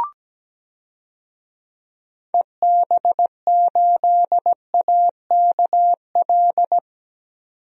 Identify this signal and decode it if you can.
Morse code, heard as a keyed tone, about 17 wpm, 700 Hz. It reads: EB8AKL
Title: DXCC-100 Call Signs - 6 Characters - 17wpm